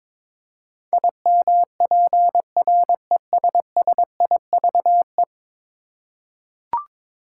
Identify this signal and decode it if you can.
Morse code, heard as a keyed tone, about 22 wpm, 700 Hz. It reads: IMPRESSIVE
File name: practice / 22wpm / 10 Letter Words - First 3 Letters Given - 22wpm